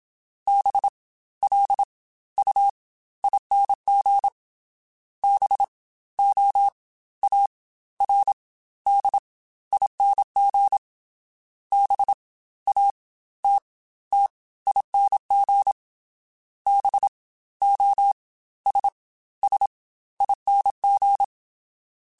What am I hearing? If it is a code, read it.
Morse code, heard as a keyed tone, about 27 wpm, spaced as for 15 wpm, 800 Hz. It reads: BLUING BOARDING BATTING BOSSING
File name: Special INGs / Special INGs-mp3s / Fast ing @25